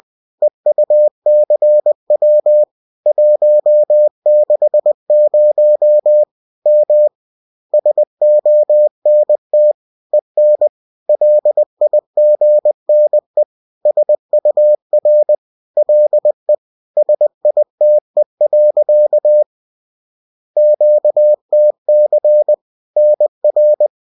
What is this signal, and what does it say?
Morse code, heard as a keyed tone, about 20 wpm, 600 Hz. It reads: EUCW 160 M SONT EN LIGNE SUR LE SITE. QTC NR